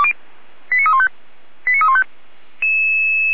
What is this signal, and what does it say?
Five-tone selective call (ZVEI), heard as quick runs of five tones, tones of about 70 ms. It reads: 89315 89315
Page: SDR/ZVEI